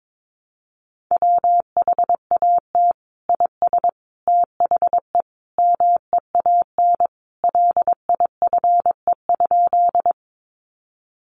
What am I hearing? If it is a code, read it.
Morse code, heard as a keyed tone, about 22 wpm, 700 Hz. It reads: WHAT IS THE MEAN LIFE?
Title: Sentences from Top 900 Words 22wpm